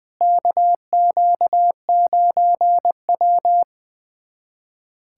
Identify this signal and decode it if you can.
Morse code, heard as a keyed tone, about 21 wpm, 700 Hz. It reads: KQ9W
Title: USA Callsign DE Callsign - 4 Characters - 20wpm